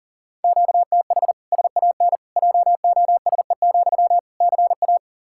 Morse code, heard as a keyed tone, about 40 wpm, 700 Hz. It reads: QTH SAN JOSE, CA
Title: Practice QSO - Exchange 1 - Beginning - 40wpm